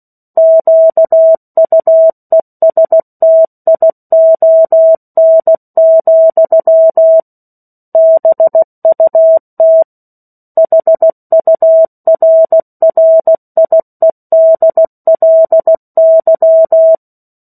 Morse code, heard as a keyed tone, about 16 wpm, 650 Hz. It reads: QUESTION, BUT HURRIEDLY
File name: 03-Chapter-03 16 WPM